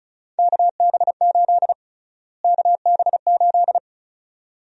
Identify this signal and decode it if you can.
Morse code, heard as a keyed tone, about 36 wpm, 700 Hz. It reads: KB8 KB8
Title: USA Call Sign Prefixes - Repeated 2x - 35wpm